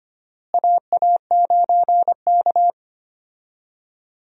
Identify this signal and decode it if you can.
Morse code, heard as a keyed tone, about 25 wpm, 700 Hz. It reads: AA9K
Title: USA Call Signs - 4 Characters - Repeated 2x - 25wpm